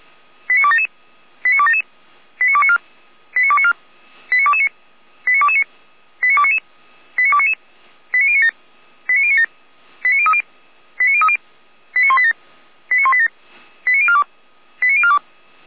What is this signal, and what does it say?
Five-tone selective call (ZVEI), heard as quick runs of five tones, tones of about 70 ms. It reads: 89277 89277 89294 89294 89229 89229 89299 89299 89097 89097 89030 89030 89187 89187 89042 89042